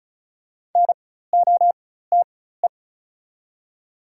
Morse code, heard as a keyed tone, about 36 wpm, spaced as for 16 wpm, 690 Hz. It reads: NOTE
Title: 4x Character Spacing - Top 500 Words - 35wpm